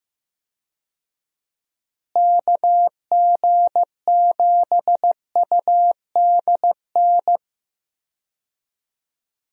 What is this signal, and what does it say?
Morse code, heard as a keyed tone, about 15 wpm, 700 Hz. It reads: KG7UDN
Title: USA Call Signs - 6 Characters - Repeated 3x - 15wpm